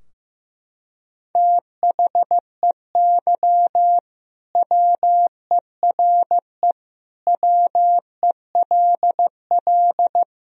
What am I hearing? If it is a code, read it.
Morse code, heard as a keyed tone, about 16 wpm, 700 Hz. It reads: THEY WERE WELL
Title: Sentences from Top 1000 Words - Review Part 3 - 15wpm